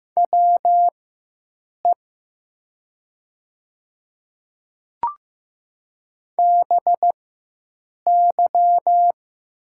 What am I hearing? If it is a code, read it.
Morse code, heard as a keyed tone, about 15 wpm, spaced as for 7 wpm, 700 Hz. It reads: WE BY
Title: 4x Character Spacing - Top 100 Words - 15wpm